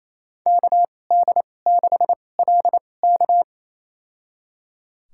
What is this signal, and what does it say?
Morse code, heard as a keyed tone, about 27 wpm, spaced as for 20 wpm, 700 Hz. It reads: KD6LK
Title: USA Call Signs - 5 Characters - 2x Character Spacing - 28wpm